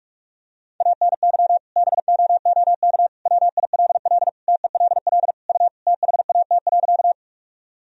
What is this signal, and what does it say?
Morse code, heard as a keyed tone, about 45 wpm, 700 Hz. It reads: ANY BOOK WILL TELL U THAT.
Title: Sentences from Top 300 Words and Abbreviations 45wpm